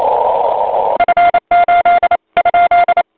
Morse code, heard as a keyed tone, about 28 wpm, 1400 Hz, with noise. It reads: F8 ?
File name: voice / 80m j3 pa5et 002 2001 08 08